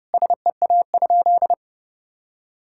Morse code, heard as a keyed tone, about 32 wpm, 700 Hz. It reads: SEA?
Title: Sentences from Top 1000 Words - Review Part 6 - 30wpm